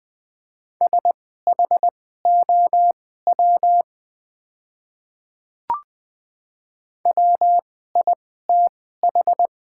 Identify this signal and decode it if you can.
Morse code, heard as a keyed tone, about 21 wpm, spaced as for 15 wpm, 700 Hz. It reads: SHOW WITH
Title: 2x Character Spacing - Top 200 Words - 20wpm